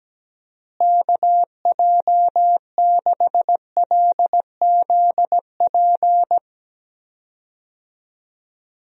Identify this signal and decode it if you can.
Morse code, heard as a keyed tone, about 17 wpm, 700 Hz. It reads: KJ6LZP
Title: USA Call Signs - 6 Characters - Repeated 2x - 17wpm